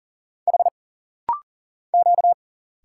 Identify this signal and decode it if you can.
Morse code, heard as a keyed tone, about 40 wpm, 700 Hz. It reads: H Q
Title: Single Letters - Rapid-Fire 40wpm